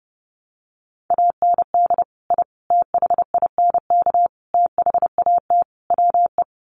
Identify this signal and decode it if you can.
Morse code, heard as a keyed tone, about 30 wpm, 700 Hz. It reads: AND I THINK THAT WE
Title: Sets of 5 Words Encore2 30wpm